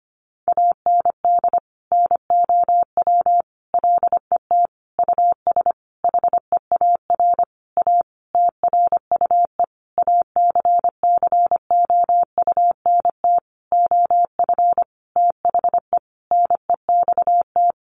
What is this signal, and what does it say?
Morse code, heard as a keyed tone, about 25 wpm, 700 Hz. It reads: AND NOW LET US HEAR A TRUE ACCOUNT OF THE NEXT
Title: Sentences from Top 900 Words 25wpm